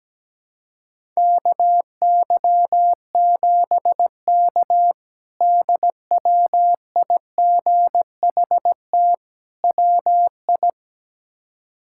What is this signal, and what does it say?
Morse code, heard as a keyed tone, about 17 wpm, 700 Hz. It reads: KY7K DWIGHT WI
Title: K1USN SST QSO Practice - 17wpm